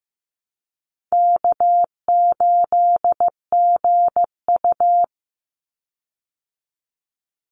Morse code, heard as a keyed tone, about 15 wpm, 700 Hz. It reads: K8GU
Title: Easy Call Signs Repeated 3x - 15wpm